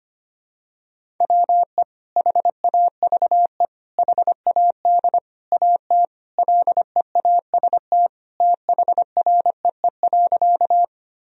Morse code, heard as a keyed tone, about 25 wpm, 700 Hz. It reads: WE HAVE HAD AT LEAST THREE.
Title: Sentences from Top 1000 Words - Review Part 1 - 25wpm